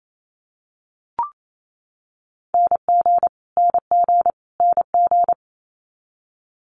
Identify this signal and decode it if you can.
Morse code, heard as a keyed tone, about 28 wpm, 700 Hz. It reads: NG NG NG
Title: Multi-letter Phonemes Repeated 3x 28wpm